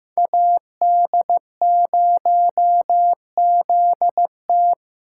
Morse code, heard as a keyed tone, about 15 wpm, 700 Hz. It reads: AD0ZT
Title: USA Call Signs - 5 Characters - 2x Character Spacing - 15wpm